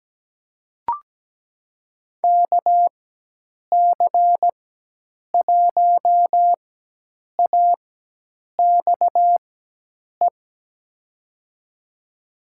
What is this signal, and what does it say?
Morse code, heard as a keyed tone, about 17 wpm, spaced as for 8 wpm, 700 Hz. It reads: KC1AXE
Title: USA Call Signs - 6 Characters - 4x Character Spacing - 17wpm